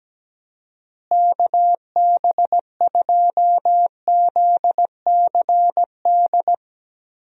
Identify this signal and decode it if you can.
Morse code, heard as a keyed tone, about 17 wpm, 700 Hz. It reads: KB2ZCD
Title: USA Call Signs - 6 Characters - Repeated 2x - 17wpm